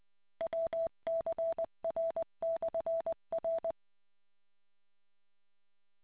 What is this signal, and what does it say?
Morse code, heard as a keyed tone, about 20 wpm, 660 Hz. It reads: WCR/R